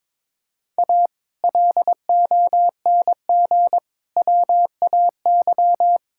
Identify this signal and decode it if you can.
Morse code, heard as a keyed tone, about 22 wpm, 700 Hz. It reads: A LONG WAY
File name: Build A Phrase - 5-Word Phrases - 22wpm